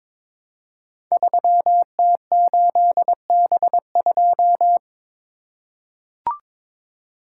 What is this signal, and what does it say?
Morse code, heard as a keyed tone, about 22 wpm, 700 Hz. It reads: 3T8B2
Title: Code Groups - 5 Letter:Numbers - 22wpm